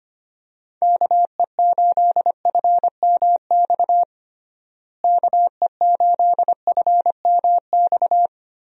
Morse code, heard as a keyed tone, about 25 wpm, 700 Hz. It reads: KE8FMX KE8FMX
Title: USA Call Signs - 6 Characters - Repeated 3x - 25wpm